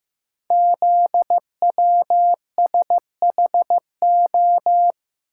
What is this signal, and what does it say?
Morse code, heard as a keyed tone, about 15 wpm, 700 Hz. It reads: ZWSHO
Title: Code Groups - 5 Letters - 15wpm